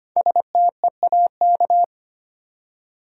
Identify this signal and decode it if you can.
Morse code, heard as a keyed tone, about 25 wpm, 700 Hz. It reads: STEAK
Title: Five Letter Words - 25wpm